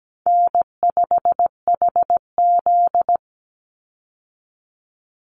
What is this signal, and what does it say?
Morse code, heard as a keyed tone, about 17 wpm, 700 Hz. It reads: N5HZ
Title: Easy Call Signs - 17wpm